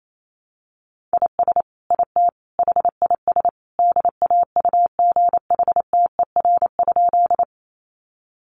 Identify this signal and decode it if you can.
Morse code, heard as a keyed tone, about 28 wpm, 700 Hz. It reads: IS IT HIS DAUGHTER?
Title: Sentences from Top 1500 Words 28wpm